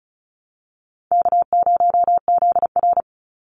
Morse code, heard as a keyed tone, about 34 wpm, 700 Hz. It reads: K0ZR
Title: Easy Call Signs Repeated 3x - 35wpm